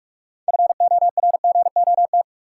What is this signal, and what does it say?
Morse code, heard as a keyed tone, about 42 wpm, 700 Hz. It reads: FORGOT
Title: Sentences from Top 1400 Words - New Word Drill Alt - 45wpm